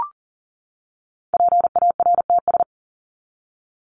Morse code, heard as a keyed tone, about 40 wpm, 700 Hz. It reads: PARTS
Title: ES:S Suffix - Top 100 Words with Base Words 40wpm